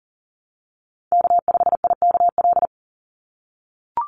Morse code, heard as a keyed tone, about 40 wpm, 700 Hz. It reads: K5IKL
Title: USA Call Signs 40wpm